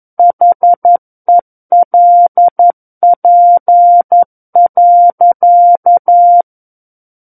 Morse code, heard as a keyed tone, about 11 wpm, 700 Hz. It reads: HELP.
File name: 11 WPM